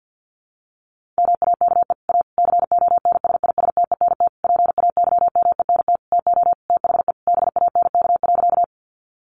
Morse code, heard as a keyed tone, about 48 wpm, 700 Hz. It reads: MAKE A CONSISTENT PAYMENT TO THE BANK.
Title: Sentences from Top 1500 Words 50wpm